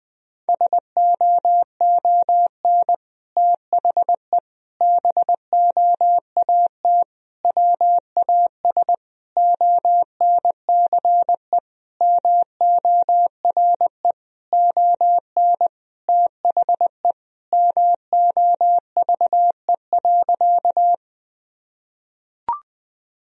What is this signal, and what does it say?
Morse code, heard as a keyed tone, about 20 wpm, 700 Hz. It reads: SOON THE BOAT WAS ONCE MORE ON THE MOVE.
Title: Sentences from Top 1000 Words - Review Part 3 - 20wpm